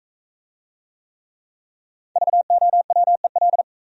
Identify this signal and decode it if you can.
Morse code, heard as a keyed tone, about 39 wpm, 700 Hz. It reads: VOWEL